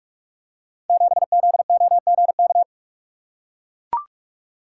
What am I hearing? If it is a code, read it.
Morse code, heard as a keyed tone, about 44 wpm, 700 Hz. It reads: 7ZOGK
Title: Code Groups - 5 Letter:Numbers - 45wpm